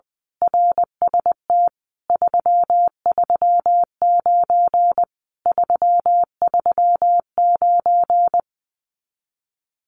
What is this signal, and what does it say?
Morse code, heard as a keyed tone, about 20 wpm, 700 Hz. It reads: RST 339 339
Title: Common QSO Phrases Encore 20wpm